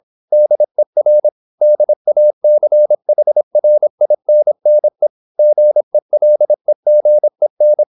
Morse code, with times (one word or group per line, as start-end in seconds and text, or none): 0.3-1.3 DER
1.6-5.1 DACHRINNE
5.4-7.8 GELEGEN